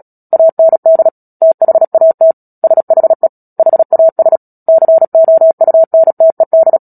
Morse code, heard as a keyed tone, about 37 wpm, 650 Hz. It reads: AND THAT SHE HAS COUNTED
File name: Book Two / 10-Chapter 10 - Epilogue - 36 WPM